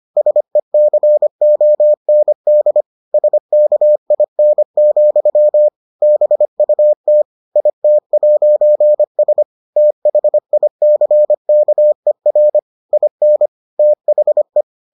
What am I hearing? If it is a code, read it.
Morse code, heard as a keyed tone, about 25 wpm, 600 Hz. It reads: SECOND SKIN, BUT IT'S THICKER IN THE